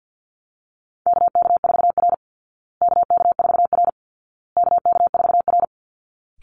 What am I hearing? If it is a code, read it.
Morse code, heard as a keyed tone, about 50 wpm, 700 Hz. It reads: KK4R KK4R KK4R
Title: Call Signs Repeated 2x - 50wpm